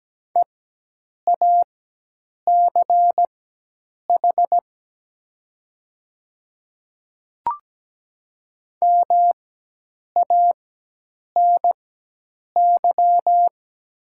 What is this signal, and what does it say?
Morse code, heard as a keyed tone, about 17 wpm, spaced as for 8 wpm, 700 Hz. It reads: EACH MANY
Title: 4x Character Spacing - Top 200 Words - 17wpm